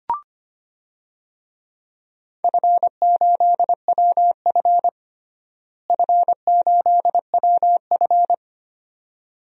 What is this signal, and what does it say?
Morse code, heard as a keyed tone, about 25 wpm, 700 Hz. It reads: F8WF F8WF
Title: DXCC-100 Call Signs - 4 Characters - Repeated 2x - 25wpm